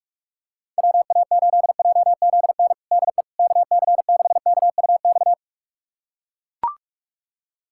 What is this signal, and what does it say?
Morse code, heard as a keyed tone, about 42 wpm, 700 Hz. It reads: WA8JZN DE KC6KUX
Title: USA Callsign DE Callsign - 6 Characters - 45wpm